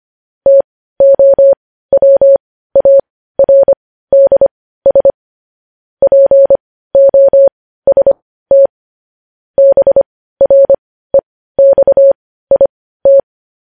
Morse code, heard as a keyed tone, about 25 wpm, spaced as for 15 wpm, 550 Hz. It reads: TOWARDS POST BREXIT
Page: News Headlines in Morse Code at 15 WPM